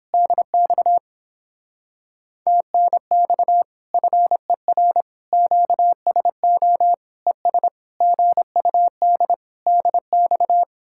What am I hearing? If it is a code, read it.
Morse code, heard as a keyed tone, about 29 wpm, 700 Hz. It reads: DX TNX FER QSO ES GUD DX